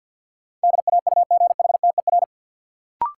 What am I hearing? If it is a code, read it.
Morse code, heard as a keyed tone, about 51 wpm, 700 Hz. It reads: DAUGHTER
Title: Sentences from Top 1500 Words - New Word Drill - 50wpm